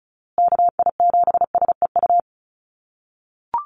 Morse code, heard as a keyed tone, about 34 wpm, 700 Hz. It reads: KI7SEU
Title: USA Call Signs Repeated 3x 35wpm